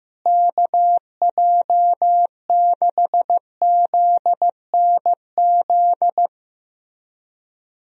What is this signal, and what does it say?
Morse code, heard as a keyed tone, about 16 wpm, 700 Hz. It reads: KJ6ZNZ
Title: USA Call Signs - 6 Characters - Repeated 3x - 15wpm